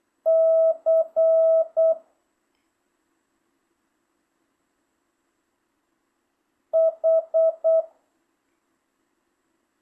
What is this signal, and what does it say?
Morse code, heard as a keyed tone, about 8 wpm, 650 Hz. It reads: C H
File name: jh8cbh / 2025cw